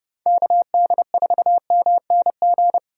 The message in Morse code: KD4MNG

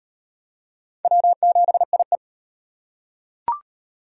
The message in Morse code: W7IE

